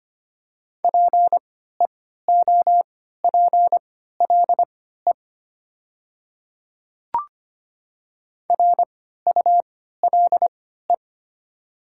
Morse code, heard as a keyed tone, about 25 wpm, spaced as for 14 wpm, 700 Hz. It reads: PEOPLE RULE
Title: 3x Character Spacing - Top 500 Words - 25wpm